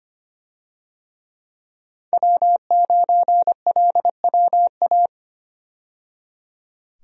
Morse code, heard as a keyed tone, about 25 wpm, 700 Hz. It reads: W9LWA
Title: USA Call Signs - 5 Characters - 25wpm